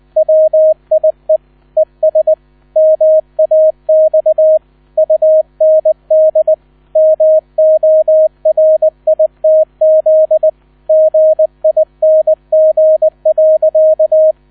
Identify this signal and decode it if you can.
Morse code, heard as a keyed tone, about 20 wpm, 630 Hz, with radio noise. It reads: WIE ES MAX UND MORITZ GING.
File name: 000 Vorwort 20wpm